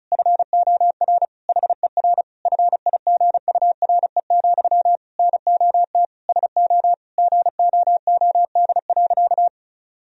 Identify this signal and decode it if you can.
Morse code, heard as a keyed tone, about 34 wpm, 700 Hz. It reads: FOR HER FIGURE, NOT SO GOOD.
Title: Sentences from Top 1000 Words - Review Part 2 - 35wpm